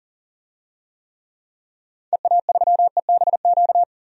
Morse code, heard as a keyed tone, about 40 wpm, 700 Hz. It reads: EA3EBQ
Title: DXCC-100 Call Signs - 6 Characters - 40wpm